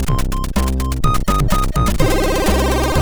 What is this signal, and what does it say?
There is no code.